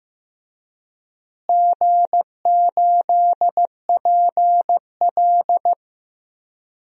Morse code, heard as a keyed tone, about 15 wpm, 700 Hz. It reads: G8PL
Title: DXCC-100 Call Signs - 4 Characters - Repeated 3x - 15wpm